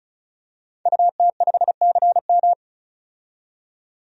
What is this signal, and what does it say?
Morse code, heard as a keyed tone, about 36 wpm, 700 Hz. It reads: UT5CM
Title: DXCC-100 Call Signs - 5 Characters - 35wpm